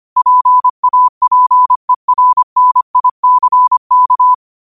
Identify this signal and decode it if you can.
Morse code, heard as a keyed tone, about 25 wpm, 1000 Hz. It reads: PAPERNICK
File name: qotdfiles / 25wpm